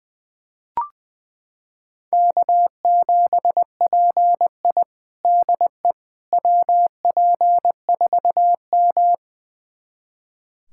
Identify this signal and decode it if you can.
Morse code, heard as a keyed tone, about 21 wpm, 700 Hz. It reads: K7PI DE WP4M
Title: USA Callsign DE Callsign - 4 Characters - 20wpm